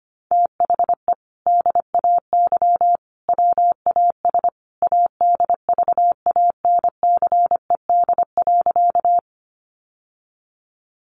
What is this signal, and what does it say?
Morse code, heard as a keyed tone, about 25 wpm, 700 Hz. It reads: THE DAY WAS ADVANCED.
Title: Sentences from Top 1200 Words 25wpm